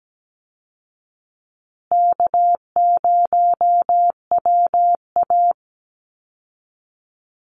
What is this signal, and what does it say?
Morse code, heard as a keyed tone, about 17 wpm, 700 Hz. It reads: K0WA
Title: Easy Call Signs Repeated 3x - 17wpm